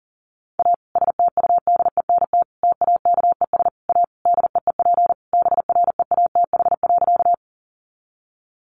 Morse code, heard as a keyed tone, about 37 wpm, 700 Hz. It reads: A STUDENT TAKES A DEEP BREATH.
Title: Additional Easy Phrases 40wpm